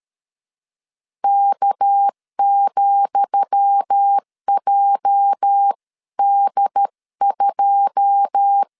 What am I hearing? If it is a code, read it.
Morse code, heard as a keyed tone, about 13 wpm, 800 Hz. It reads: K,JD2
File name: LEZIONI / 10 SIGNS / 10D COMA 12wpm 2544